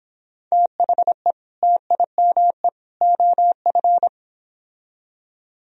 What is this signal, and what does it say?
Morse code, heard as a keyed tone, about 25 wpm, 700 Hz. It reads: THE TIME OF